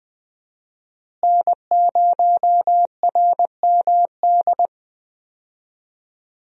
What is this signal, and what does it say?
Morse code, heard as a keyed tone, about 20 wpm, 700 Hz. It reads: N0RMD
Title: USA Call Signs - 5 Characters - Repeated 3x - 20wpm